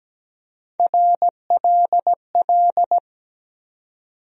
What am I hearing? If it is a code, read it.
Morse code, heard as a keyed tone, about 17 wpm, 700 Hz. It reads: RLL
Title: USA Call Sign Postfixes - 3 Letters 17wpm